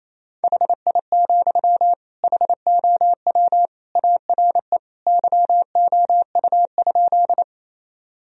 Morse code, heard as a keyed tone, about 28 wpm, 700 Hz. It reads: HI, HOW ARE YOU?
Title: Sentences from Top 1000 Words - Review Part 3 - 28wpm